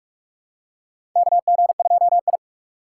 Morse code, heard as a keyed tone, about 42 wpm, 700 Hz. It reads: KG2I
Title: USA Call Signs - 4 Characters - 3x Character Spacing - 45wpm